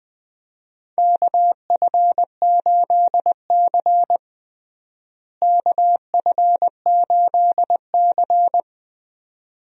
Morse code, heard as a keyed tone, about 21 wpm, 700 Hz. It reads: KF8C KF8C
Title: USA Call Signs - 4 Characters - Repeated 2x - 20wpm